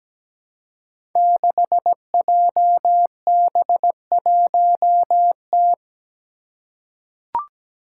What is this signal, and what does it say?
Morse code, heard as a keyed tone, about 17 wpm, 700 Hz. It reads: 6JB1T
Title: Five Challenging Letter-Number Combinations 17wpm